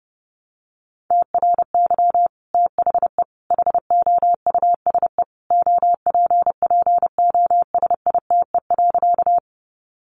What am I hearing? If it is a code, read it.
Morse code, heard as a keyed tone, about 28 wpm, 700 Hz. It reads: TRY THE HOUSE OPPOSITE.
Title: Sentences from Top 1300 Words 30wpm